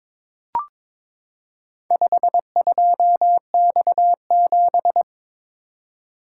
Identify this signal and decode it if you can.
Morse code, heard as a keyed tone, about 22 wpm, 700 Hz. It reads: 52X7
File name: practice / 22wpm / Four Letter-Number Combinations - 22wpm